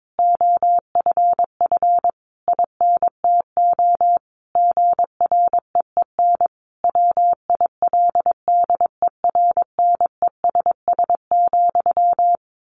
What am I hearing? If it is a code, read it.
Morse code, heard as a keyed tone, about 22 wpm, 700 Hz. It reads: OFF INTO GREEN WILDERNESS,
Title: A Sound Of Thunder 22wpm